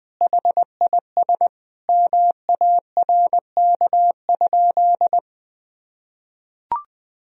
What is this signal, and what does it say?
Morse code, heard as a keyed tone, about 20 wpm, 700 Hz. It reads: HIS MARK?
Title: Sentences from Top 300 Words and Abbreviations 20wpm